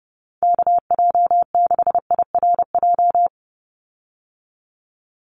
Morse code, heard as a keyed tone, about 30 wpm, 700 Hz. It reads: KJ6IRJ
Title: USA Call Signs 30wpm